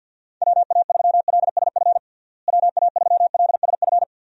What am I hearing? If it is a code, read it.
Morse code, heard as a keyed tone, about 52 wpm, 700 Hz. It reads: WA3LSF WA3LSF
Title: USA Call Signs - 6 Characters - Repeated 2x - 50wpm